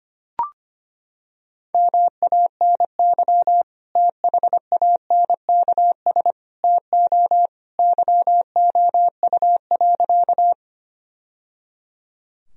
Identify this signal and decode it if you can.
Morse code, heard as a keyed tone, about 25 wpm, 700 Hz. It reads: MANY THANKS TO YOU.
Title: Sentences from Top 1000 Words - Review Part 3 - 25wpm